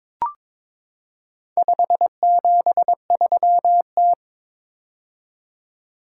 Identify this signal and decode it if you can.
Morse code, heard as a keyed tone, about 22 wpm, 700 Hz. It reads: 573T
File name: practice / 22wpm / Four Letter-Number Combinations - 22wpm